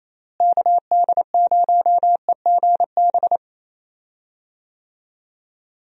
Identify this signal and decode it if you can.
Morse code, heard as a keyed tone, about 28 wpm, 700 Hz. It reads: KD0EGB
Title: USA Call Signs - 6 Characters - Repeated 2x - 28wpm